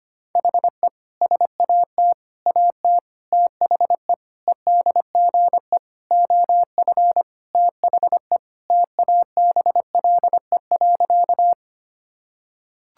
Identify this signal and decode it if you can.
Morse code, heard as a keyed tone, about 25 wpm, 700 Hz. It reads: HE SAT AT THE EDGE OF THE TABLE.
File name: practice / 25wpm / Sentences with 3-Word Phrases - 25wpm